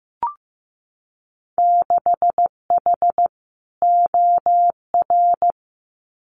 Morse code, heard as a keyed tone, about 15 wpm, 700 Hz. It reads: 6H OR
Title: Winter Field Day Exchanges Encore 15wpm